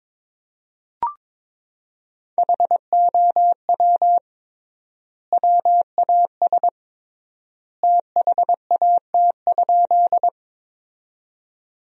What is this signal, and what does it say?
Morse code, heard as a keyed tone, about 22 wpm, 700 Hz. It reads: HOW WAS THAT?
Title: 3x Word-Spacing - Sentences from Top 100 Words - 22wpm